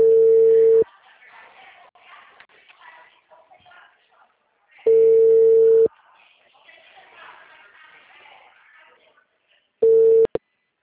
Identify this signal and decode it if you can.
Ringback tone (China, 450 Hz), with speech.
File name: out- -2038- - - .3103